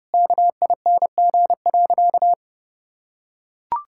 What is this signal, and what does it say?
Morse code, heard as a keyed tone, about 32 wpm, 700 Hz. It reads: KING.
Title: Sentences from Top 1000 Words - Review Part 5 - 30wpm